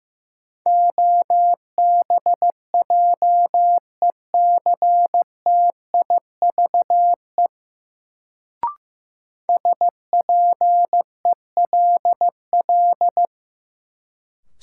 Morse code, heard as a keyed tone, about 15 wpm, 700 Hz. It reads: OBJECTIVE SPELL
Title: Sentences from Top 1200 Words - New Word Drill Alt - 15wpm